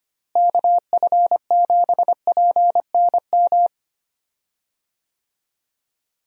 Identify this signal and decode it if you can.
Morse code, heard as a keyed tone, about 25 wpm, 700 Hz. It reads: KF7PNM
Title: USA Call Signs - 6 Characters - Repeated 2x - 25wpm